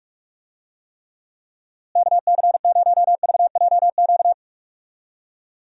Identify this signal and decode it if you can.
Morse code, heard as a keyed tone, about 46 wpm, 700 Hz. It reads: KC0VJQ
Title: USA Call Signs - 6 Characters - Repeated 3x - 45wpm